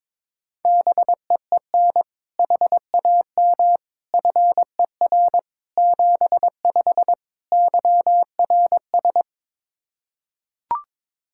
Morse code, heard as a keyed tone, about 22 wpm, 700 Hz. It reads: BEEN HAM FER 75 YRS
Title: Practice QSO - Exchange 3 - More Info - 22wpm